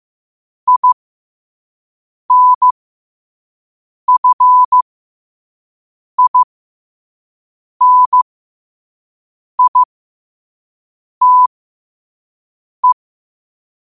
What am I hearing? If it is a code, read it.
Morse code, heard as a keyed tone, about 15 wpm, spaced as for 5 wpm, 1000 Hz. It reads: INFINITE
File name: qotdfiles / 5wpm